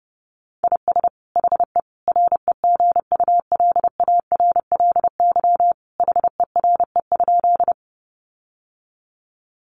Morse code, heard as a keyed tone, about 30 wpm, 700 Hz. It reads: IS HE REGULARLY HERE?
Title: Sentences from Top 1400 Words 30wpm